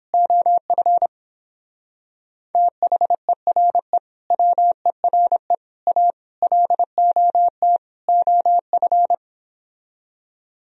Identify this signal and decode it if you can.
Morse code, heard as a keyed tone, about 29 wpm, 700 Hz. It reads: OF THERE WERE A LOT OF